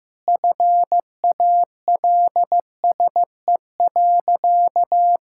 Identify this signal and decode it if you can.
Morse code, heard as a keyed tone, about 15 wpm, 700 Hz. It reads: FALSE.
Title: Sentences from Top 1600 Words 15wpm